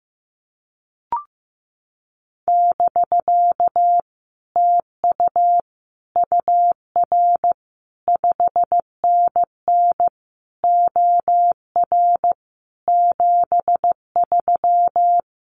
Morse code, heard as a keyed tone, about 15 wpm, 700 Hz. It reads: <BK> TU UR 5NN OR 73
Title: Common QSO Phrases Encore 15wpm